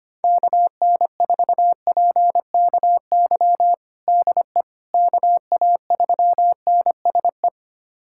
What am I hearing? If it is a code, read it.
Morse code, heard as a keyed tone, about 25 wpm, 700 Hz. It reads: KN4PKY DE KA3NSE
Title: USA Callsign DE Callsign - 6 Characters - 25wpm